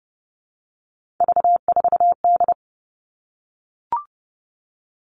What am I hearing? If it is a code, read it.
Morse code, heard as a keyed tone, about 30 wpm, 700 Hz. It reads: V4D
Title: All Cut Numbers - 3-Digit 30wpm